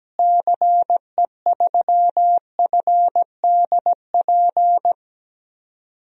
Morse code, heard as a keyed tone, about 17 wpm, 700 Hz. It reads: CE3FDP
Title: DXCC-100 Call Signs - 6 Characters - Repeated 2x - 17wpm